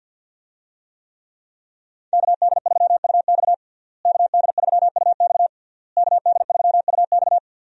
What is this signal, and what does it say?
Morse code, heard as a keyed tone, about 52 wpm, 700 Hz. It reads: KD3UX KD3UX KD3UX
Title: USA Call Signs - 5 Characters - Repeated 3x - 50wpm